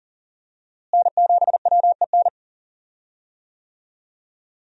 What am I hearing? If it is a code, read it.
Morse code, heard as a keyed tone, about 39 wpm, 700 Hz. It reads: N7WEN
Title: USA Call Signs - 5 Characters - Repeated 3x - 40wpm